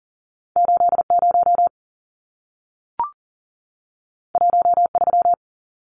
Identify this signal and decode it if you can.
Morse code, heard as a keyed tone, about 40 wpm, 700 Hz. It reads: 80 13
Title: Numbers - 2-Digit 40wpm